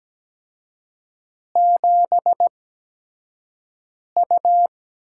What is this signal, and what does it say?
Morse code, heard as a keyed tone, about 17 wpm, spaced as for 5 wpm, 700 Hz. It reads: 7U
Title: USA Call Signs - 8x Character Spacing - 17wpm